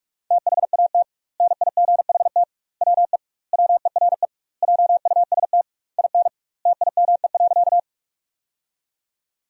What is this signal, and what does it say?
Morse code, heard as a keyed tone, about 46 wpm, 700 Hz. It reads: THAT NIGHT WE WERE JUST IN TIME.
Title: 2x Word-Spacing - Sentences from Top 300 Words - 45wpm